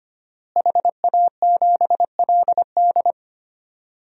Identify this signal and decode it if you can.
Morse code, heard as a keyed tone, about 25 wpm, 700 Hz. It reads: HA7LD
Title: DXCC-100 Call Signs - 5 Characters - Repeated 3x - 25wpm